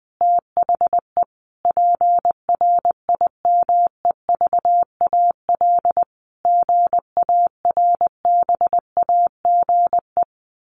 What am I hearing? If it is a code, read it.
Morse code, heard as a keyed tone, about 19 wpm, 700 Hz. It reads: THE PRIMEVAL GARBAGE
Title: A Sound Of Thunder 20wpm